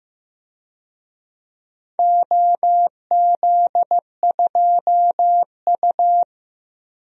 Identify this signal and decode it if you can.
Morse code, heard as a keyed tone, about 15 wpm, 700 Hz. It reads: OZ2U
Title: DXCC-100 Call Signs - 4 Characters - 15wpm